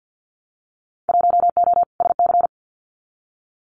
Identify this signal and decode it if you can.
Morse code, heard as a keyed tone, about 50 wpm, 700 Hz. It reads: 1O SC
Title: Winter Field Day Exchanges 50wpm